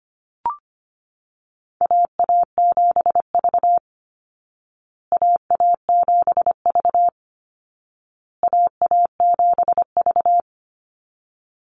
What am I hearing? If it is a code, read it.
Morse code, heard as a keyed tone, about 25 wpm, 700 Hz. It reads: AA7V AA7V AA7V
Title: Call Signs Repeated 2x - 25wpm